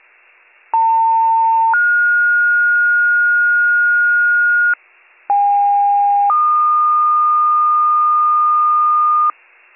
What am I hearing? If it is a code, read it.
Two-tone page, heard as a one-second tone, then a three-second tone, about 900 then 1430 Hz, about 810 then 1220 Hz, with noise. It is heard from 0.7 to 9.3 s.